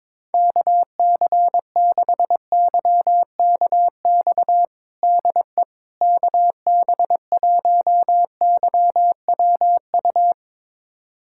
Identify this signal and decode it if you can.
Morse code, heard as a keyed tone, about 22 wpm, 700 Hz. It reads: KC6YKX DE KB1YWU
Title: USA Callsign DE Callsign - 6 Characters - 22wpm